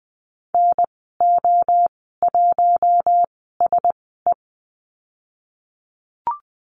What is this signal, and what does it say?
Morse code, heard as a keyed tone, about 20 wpm, spaced as for 15 wpm, 700 Hz. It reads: NO1SE
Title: 069 Lesson 12g - Letter d - Call Signs 20wpm